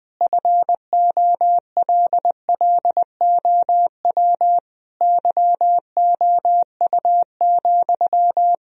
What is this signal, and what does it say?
Morse code, heard as a keyed tone, about 20 wpm, 700 Hz. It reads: FOLLOW YOU,
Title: Sentences from Top 1000 Words - Review Part 4 - 20wpm